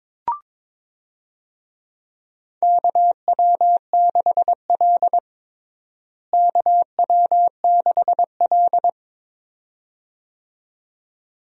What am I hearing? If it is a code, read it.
Morse code, heard as a keyed tone, about 22 wpm, 700 Hz. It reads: KW6L KW6L
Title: USA Call Signs - 4 Characters - Repeated 2x - 22wpm